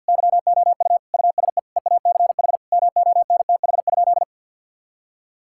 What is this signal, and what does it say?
Morse code, heard as a keyed tone, about 50 wpm, 700 Hz. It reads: YOU USE EACH MONTH?
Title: Sentences from Top 1000 Words - Review Part 6 - 50wpm